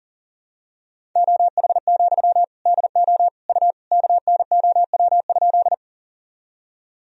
Morse code, heard as a keyed tone, about 40 wpm, 700 Hz. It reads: OH, DO U KNOW?
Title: Sentences from Top 500 Words and Abbreviations 40wpm